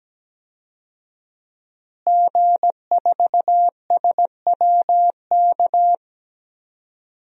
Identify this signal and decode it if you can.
Morse code, heard as a keyed tone, about 17 wpm, 700 Hz. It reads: G4SWK
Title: DXCC-100 Call Signs - 5 Characters - Repeated 3x - 17wpm